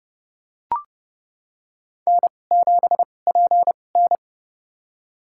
Morse code, heard as a keyed tone, about 30 wpm, spaced as for 22 wpm, 700 Hz. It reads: N7PN
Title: USA Call Signs - 4 Characters - 2x Character Spacing - 30wpm